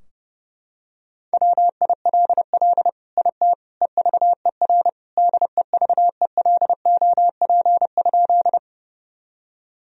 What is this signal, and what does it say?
Morse code, heard as a keyed tone, about 32 wpm, 700 Hz. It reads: WILL IT EVER DEVELOP?
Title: Sentences from Top 1000 Words - Review Part 2 - 30wpm